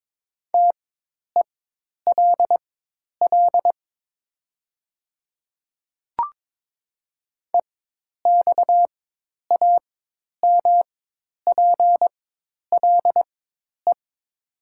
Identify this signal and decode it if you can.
Morse code, heard as a keyed tone, about 22 wpm, spaced as for 10 wpm, 700 Hz. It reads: TELL EXAMPLE
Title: 4x Character Spacing - Top 300 Words - 22wpm